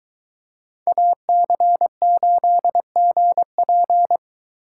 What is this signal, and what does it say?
Morse code, heard as a keyed tone, about 22 wpm, 700 Hz. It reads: AC8GP